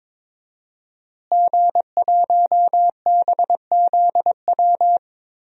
Morse code, heard as a keyed tone, about 22 wpm, 700 Hz. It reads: G1BZW
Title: DXCC-100 Call Signs - 5 Characters - Repeated 3x - 22wpm